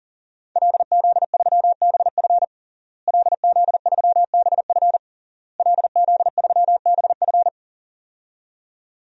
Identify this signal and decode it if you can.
Morse code, heard as a keyed tone, about 40 wpm, 700 Hz. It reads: LZ3BF LZ3BF LZ3BF
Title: DXCC-100 Call Signs - 5 Characters - Repeated 3x - 40wpm